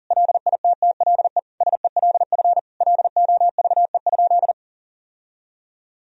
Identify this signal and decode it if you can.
Morse code, heard as a keyed tone, about 37 wpm, 700 Hz. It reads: LITTLE SELF LOVE?
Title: Sentences from Top 1000 Words - Review Part 1 - 40wpm